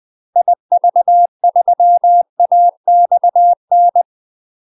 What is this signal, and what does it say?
Morse code, heard as a keyed tone, about 20 wpm, 700 Hz. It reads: IV3AXN
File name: UK MASTER QSOs TRAINING / 20WPM / UK MASTER QSOs 0036